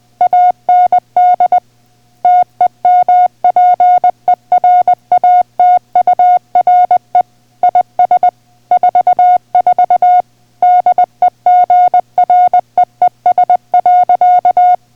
Morse code, heard as a keyed tone, about 20 wpm, 720 Hz. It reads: AND TEMPERATURE IS 44 DEGREES.